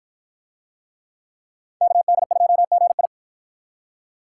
Morse code, heard as a keyed tone, about 53 wpm, 700 Hz. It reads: KD2GI